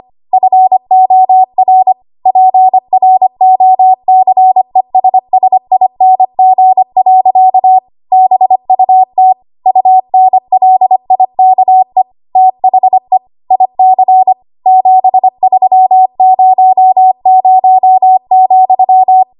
Morse code, heard as a keyed tone, about 25 wpm, 750 Hz. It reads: FOR PROCESSING. BUT UNLIKE THE IC 7300,
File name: Archive / 25 WPM / 25WPM